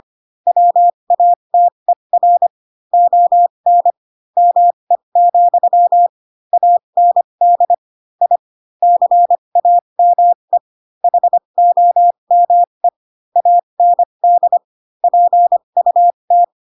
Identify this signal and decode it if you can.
Morse code, heard as a keyed tone, about 25 wpm, spaced as for 22 wpm, 700 Hz. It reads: WATER ON ME, AND I CAME HOME AND PUT
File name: Pinocchio / Pinocchio ch7 18 25 ik1hge